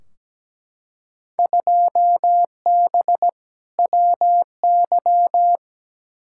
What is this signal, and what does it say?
Morse code, heard as a keyed tone, about 17 wpm, 700 Hz. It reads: 2B WY
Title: Field Day Exchanges Encore - 17wpm